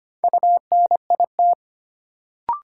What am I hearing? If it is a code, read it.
Morse code, heard as a keyed tone, about 25 wpm, 700 Hz. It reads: UNIT
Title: Sentences from Top 1000 Words - Review Part 2 - New Word Drill - 25wpm